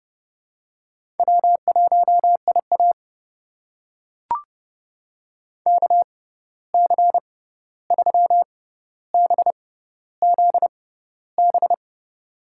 Morse code, heard as a keyed tone, about 30 wpm, spaced as for 14 wpm, 700 Hz. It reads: W1IA KC3BZB
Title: USA Call Signs - 6x Character Spacing - 30wpm